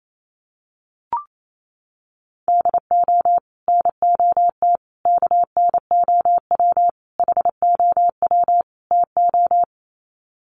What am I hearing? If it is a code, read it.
Morse code, heard as a keyed tone, about 28 wpm, 700 Hz. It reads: DO NOT KNOW HOW TO
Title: Sets of 5 Words 28wpm